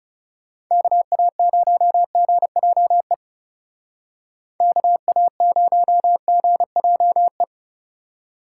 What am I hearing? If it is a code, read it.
Morse code, heard as a keyed tone, about 36 wpm, 700 Hz. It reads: KA0GJE KA0GJE
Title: Speed Racing - USA Call Signs 30wpm